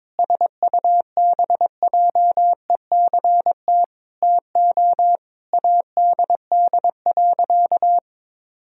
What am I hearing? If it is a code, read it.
Morse code, heard as a keyed tone, about 22 wpm, 700 Hz. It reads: SUBJECT TO ADD.
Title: Sentences from Top 1000 Words - Review Part 1 - 22wpm